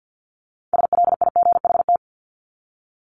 Morse code, heard as a keyed tone, about 48 wpm, 700 Hz. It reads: SLIGHT
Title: LY Suffix - Top 100 Words with Base Words 50wpm